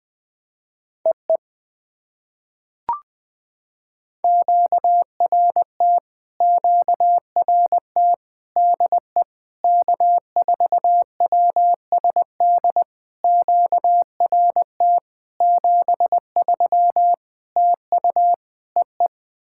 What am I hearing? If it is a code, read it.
Morse code, heard as a keyed tone, about 20 wpm, 700 Hz. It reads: EE QRT QRT DE K4WSD QRT 73 TU EE
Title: POTA QSO Practice - 20wpm